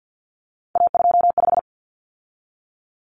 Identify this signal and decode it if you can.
Morse code, heard as a keyed tone, about 50 wpm, 700 Hz. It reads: A25
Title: Common Cut Numbers - 3-Digit 50wpm